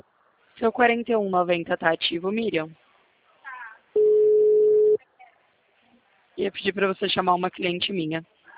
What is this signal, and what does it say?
Ringback tone (Europe, 425 Hz), with speech.